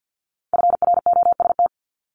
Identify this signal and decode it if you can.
Morse code, heard as a keyed tone, about 50 wpm, 700 Hz. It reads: FROST
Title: Multi-letter Phonemes Repeated 3x 50wpm